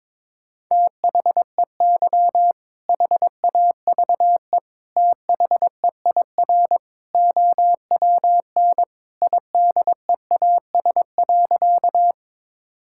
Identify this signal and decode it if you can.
Morse code, heard as a keyed tone, about 21 wpm, 700 Hz. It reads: THEY HAVE THEIR OWN IDEAS.
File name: practice / 22wpm / Sentences with 2-Word Phrases - 22wpm